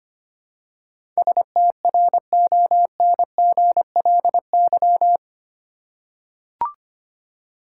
Morse code, heard as a keyed tone, about 25 wpm, 700 Hz. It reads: STRONGLY
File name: Eight Letter Words - 25wpm